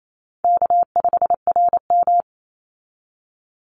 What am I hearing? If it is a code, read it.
Morse code, heard as a keyed tone, about 28 wpm, 700 Hz. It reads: K5RM
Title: Easy Call Signs Repeated 3x - 28wpm